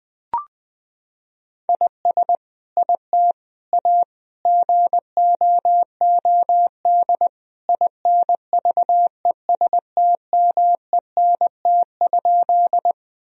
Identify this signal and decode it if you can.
Morse code, heard as a keyed tone, about 20 wpm, 700 Hz. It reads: IS IT A GOOD INVESTMENT?
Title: Sentences from Top 1000 Words - Review Part 4 - 20wpm